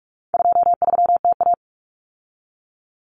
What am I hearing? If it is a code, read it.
Morse code, heard as a keyed tone, about 44 wpm, 700 Hz. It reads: 23TA
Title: Common Cut Numbers - 4-Digit 45wpm